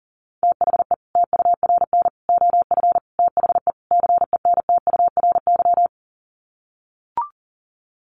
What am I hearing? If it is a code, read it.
Morse code, heard as a keyed tone, about 40 wpm, 700 Hz. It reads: THE TURN OF THE CENTURY
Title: Sets of 5 Words 40wpm